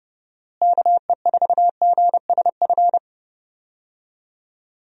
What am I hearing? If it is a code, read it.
Morse code, heard as a keyed tone, about 31 wpm, 700 Hz. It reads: KE4GSF
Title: USA Call Signs - 6 Characters - Repeated 2x - 30wpm